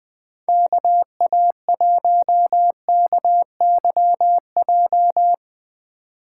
Morse code, heard as a keyed tone, about 20 wpm, 700 Hz. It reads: KA1KYJ
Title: Speed Racing - USA Call Signs 17wpm